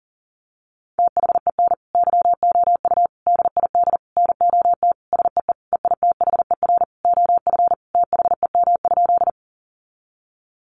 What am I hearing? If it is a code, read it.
Morse code, heard as a keyed tone, about 40 wpm, 700 Hz. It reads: THEN YOU DID NOT SEE EITHER OF THEM?
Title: Sentences from Top 600 Words 40wpm